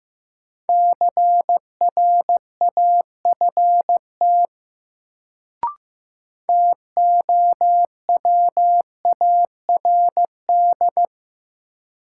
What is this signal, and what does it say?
Morse code, heard as a keyed tone, about 15 wpm, 700 Hz. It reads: CRAFT TOWARD
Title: Sentences from Top 1000 Words - Review Part 6 - New Word Drill - 15wpm